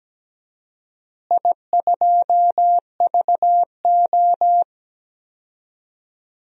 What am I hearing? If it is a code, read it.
Morse code, heard as a keyed tone, about 17 wpm, 700 Hz. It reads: I2VO
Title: DXCC-100 Call Signs - 4 Characters - Repeated 2x - 17wpm